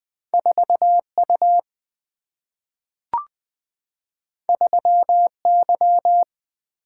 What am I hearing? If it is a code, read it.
Morse code, heard as a keyed tone, about 20 wpm, 700 Hz. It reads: 4 U 3 Y
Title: Single-Letter Single-Number 20wpm